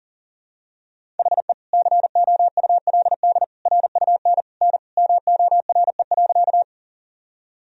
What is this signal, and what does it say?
Morse code, heard as a keyed tone, about 40 wpm, 700 Hz. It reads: HE COULD RUN N MORE.
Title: Sentences from Top 300 Words and Abbreviations 40wpm